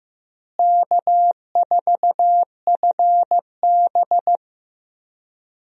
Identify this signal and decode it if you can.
Morse code, heard as a keyed tone, about 15 wpm, 700 Hz. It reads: K4FB
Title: USA Call Signs - 4 Characters - Repeated 3x - 15wpm